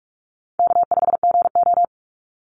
Morse code, heard as a keyed tone, about 44 wpm, 700 Hz. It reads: K5GO
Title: Easy Call Signs Repeated 3x - 45wpm